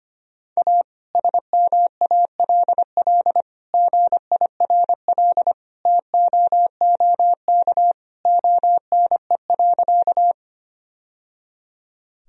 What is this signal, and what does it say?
Morse code, heard as a keyed tone, about 25 wpm, 700 Hz. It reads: A SMALL GIRL TOOK ONE.
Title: Sentences from Top 1000 Words - Review Part 1 - 25wpm